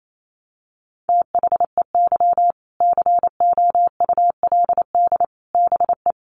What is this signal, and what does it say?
Morse code, heard as a keyed tone, about 28 wpm, 700 Hz. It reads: THEY COULD BE
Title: Sentences from Top 1200 Words 28wpm